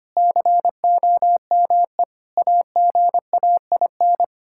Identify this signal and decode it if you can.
Morse code, heard as a keyed tone, about 25 wpm, 700 Hz. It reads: COME AGAIN
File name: practice / 25wpm / Sentences with 4-Word Phrases - 25wpm